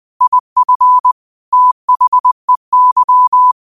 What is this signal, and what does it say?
Morse code, heard as a keyed tone, about 20 wpm, 1000 Hz. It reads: IF THEY